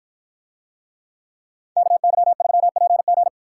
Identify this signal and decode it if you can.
Morse code, heard as a keyed tone, about 53 wpm, 700 Hz. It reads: KY3PG